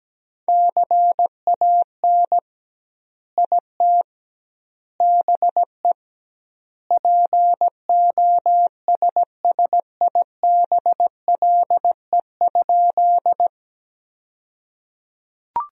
Morse code, heard as a keyed tone, about 17 wpm, 700 Hz. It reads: CAN IT BE POSSIBLE?
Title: 2x Word-Spacing - Sentences from Top 500 Words - 17wpm